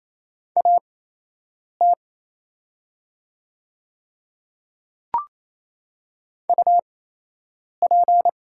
Morse code, heard as a keyed tone, about 28 wpm, spaced as for 8 wpm, 700 Hz. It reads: AT UP
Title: 8x Character Spacing - Top 100 Words - 28wpm